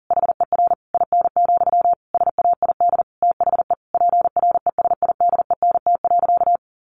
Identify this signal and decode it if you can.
Morse code, heard as a keyed tone, about 40 wpm, 700 Hz. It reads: HER IN, SAID THE PRESIDENT.
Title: Sentences from Top 1000 Words 40wpm